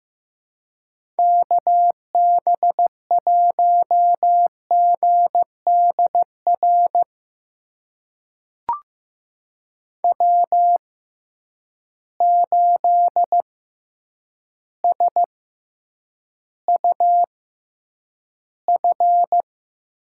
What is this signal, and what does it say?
Morse code, heard as a keyed tone, about 15 wpm, spaced as for 8 wpm, 700 Hz. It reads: KB1GDR W8SUF
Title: USA Call Signs - 6x Character Spacing - 15wpm